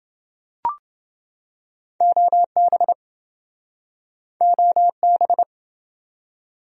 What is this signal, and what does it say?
Morse code, heard as a keyed tone, about 28 wpm, 700 Hz. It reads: OB OB